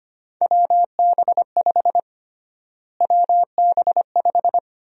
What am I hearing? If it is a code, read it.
Morse code, heard as a keyed tone, about 25 wpm, 700 Hz. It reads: WB5 WB5
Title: USA Call Sign Prefixes - Repeated 2x - 25wpm